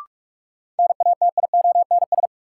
Morse code, heard as a keyed tone, about 43 wpm, 700 Hz. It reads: NATIONS
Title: Sets of 3 Words - New Word Drill - 45wpm